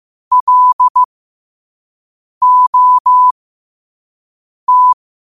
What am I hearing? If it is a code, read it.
Morse code, heard as a keyed tone, about 15 wpm, spaced as for 5 wpm, 1000 Hz. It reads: LOT
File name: qotdfiles / 5wpm